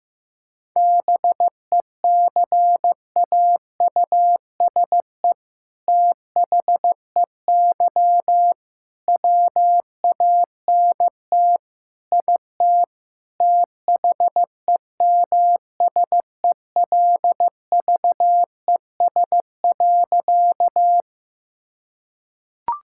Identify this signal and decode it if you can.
Morse code, heard as a keyed tone, about 15 wpm, 700 Hz. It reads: BECAUSE THEY WANT IT THEMSELVES.
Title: Sentences from Top 1000 Words - Review Part 5 - 15wpm